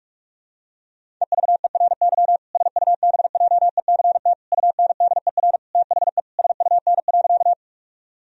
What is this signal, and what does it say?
Morse code, heard as a keyed tone, about 46 wpm, 700 Hz. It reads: EVERY SUBJECT UNDER THE SUN.
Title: Sentences from Top 1000 Words - Review Part 1 - 45wpm